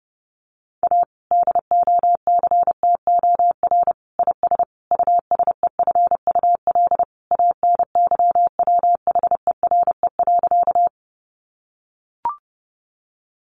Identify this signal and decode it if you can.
Morse code, heard as a keyed tone, about 30 wpm, 700 Hz. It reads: A DOCTOR IS USEFUL ANYWHERE.
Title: Sentences from Top 1500 Words 30wpm